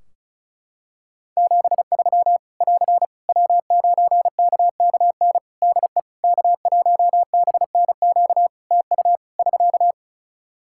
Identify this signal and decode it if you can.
Morse code, heard as a keyed tone, about 34 wpm, 700 Hz. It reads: 73 <AR> W9KKN DE K1BNQ TU <SK>
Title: K1USN SST QSO Practice 35wpm